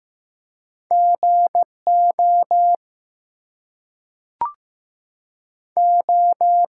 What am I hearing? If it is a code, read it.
Morse code, heard as a keyed tone, about 15 wpm, 700 Hz. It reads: GO O
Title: Two Letter Words - 15wpm